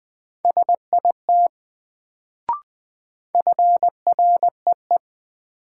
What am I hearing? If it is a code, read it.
Morse code, heard as a keyed tone, about 21 wpm, 700 Hz. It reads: SIT FREE
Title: Sentences from Top 500 Words - New Word Drill - 20wpm